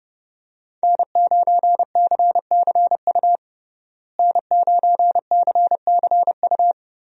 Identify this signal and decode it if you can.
Morse code, heard as a keyed tone, about 30 wpm, 700 Hz. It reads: N9CCU N9CCU
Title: USA Call Signs - 5 Characters - Repeated 3x - 30wpm